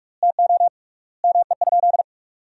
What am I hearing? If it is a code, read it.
Morse code, heard as a keyed tone, about 44 wpm, 700 Hz. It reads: TO ME?
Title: 3x Word-Spacing - Sentences from Top 200 Words - 45wpm